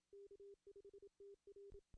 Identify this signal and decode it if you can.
Morse code, heard as a keyed tone, about 27 wpm, 400 Hz. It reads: K5TR
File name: CW / cqww-cw-k5tr